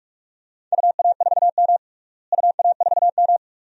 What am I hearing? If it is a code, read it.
Morse code, heard as a keyed tone, about 42 wpm, 700 Hz. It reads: UA4M UA4M
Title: DXCC-100 Call Signs - 4 Characters - Repeated 2x - 45wpm